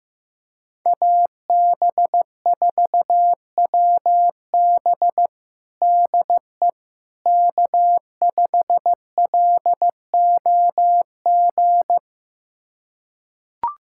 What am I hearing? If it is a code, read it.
Morse code, heard as a keyed tone, about 16 wpm, 700 Hz. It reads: AB4WB DE K5LOG
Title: USA Callsign DE Callsign - 5 Characters - 15wpm